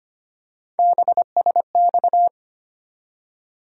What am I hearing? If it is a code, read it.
Morse code, heard as a keyed tone, about 25 wpm, 700 Hz. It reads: BSX
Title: USA Call Sign Postfixes - 3 Letters 25wpm